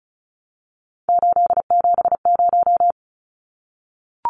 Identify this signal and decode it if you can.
Morse code, heard as a keyed tone, about 34 wpm, 700 Hz. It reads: 870
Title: Numbers - 3-Digit 35wpm